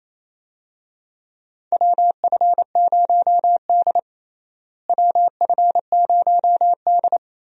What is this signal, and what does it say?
Morse code, heard as a keyed tone, about 28 wpm, 700 Hz. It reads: WF0D WF0D
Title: USA Call Signs - 4 Characters - Repeated 3x - 28wpm